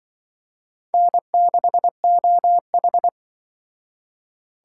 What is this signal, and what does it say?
Morse code, heard as a keyed tone, about 23 wpm, 700 Hz. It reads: N6OH
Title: Speed Racing - USA 1x2 Call Signs 20wpm